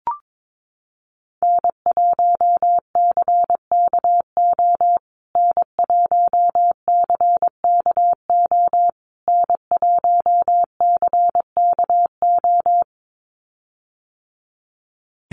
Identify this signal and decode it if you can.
Morse code, heard as a keyed tone, about 22 wpm, 700 Hz. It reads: N1CKO N1CKO N1CKO
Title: USA Call Signs Repeated 3x 22wpm